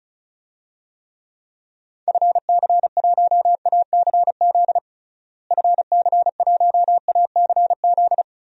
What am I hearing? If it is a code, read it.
Morse code, heard as a keyed tone, about 36 wpm, 700 Hz. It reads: FC1ACZ FC1ACZ
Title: DXCC-100 Call Signs - 6 Characters - Repeated 2x - 35wpm